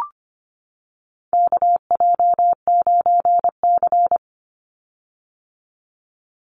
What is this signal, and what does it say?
Morse code, heard as a keyed tone, about 25 wpm, 700 Hz. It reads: KJ9C
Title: Call Signs - 25wpm